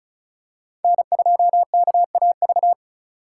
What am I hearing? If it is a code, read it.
Morse code, heard as a keyed tone, about 34 wpm, 700 Hz. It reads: N2KAV
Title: USA Call Signs - 5 Characters - 3x Character Spacing - 35wpm